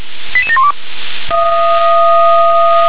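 Five-tone selective call (ZVEI), heard as quick runs of five tones, tones of about 70 ms. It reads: 88712
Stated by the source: ZVEI five-tone selective call 88692, received by SDR.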